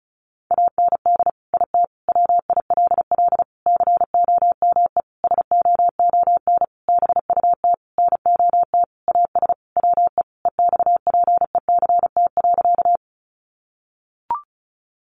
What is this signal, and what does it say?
Morse code, heard as a keyed tone, about 34 wpm, 700 Hz. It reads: AND IT WILL COME SOON BUT NOT AS WE EXPECT.
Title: Sentences from Top 1100 Words 35wpm